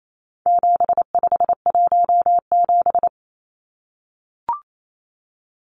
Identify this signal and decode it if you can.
Morse code, heard as a keyed tone, about 28 wpm, 700 Hz. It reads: 7517
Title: Numbers - 4-Digit 28wpm